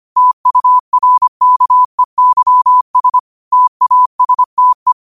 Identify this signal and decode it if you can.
Morse code, heard as a keyed tone, about 25 wpm, 1000 Hz. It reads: TURKEYS TASTE
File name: 25wpm